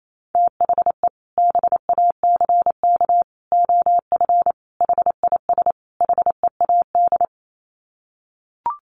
Morse code, heard as a keyed tone, about 28 wpm, 700 Hz. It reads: THE BACK OF HIS HEAD
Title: Sets of 5 Words Encore 28wpm